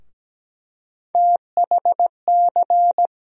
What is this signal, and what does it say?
Morse code, heard as a keyed tone, about 17 wpm, 700 Hz. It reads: THC
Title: USA Call Sign Postfixes - 3 Letters 17wpm